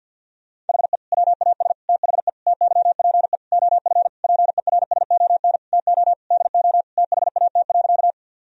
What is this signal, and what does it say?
Morse code, heard as a keyed tone, about 50 wpm, 700 Hz. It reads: HE WAS THE TYPE OF PERSON TO DO THAT.
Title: Sentences from Top 1000 Words - Review Part 3 - 50wpm